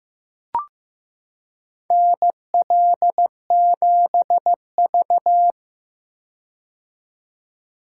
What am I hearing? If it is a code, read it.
Morse code, heard as a keyed tone, about 15 wpm, 700 Hz. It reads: NL7V
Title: K1USN SST QSO Practice - 15wpm